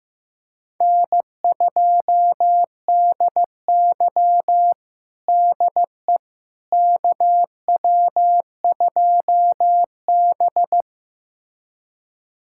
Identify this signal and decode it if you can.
Morse code, heard as a keyed tone, about 16 wpm, 700 Hz. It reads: N2DY DE KW2B
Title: USA Callsign DE Callsign - 4 Characters - 15wpm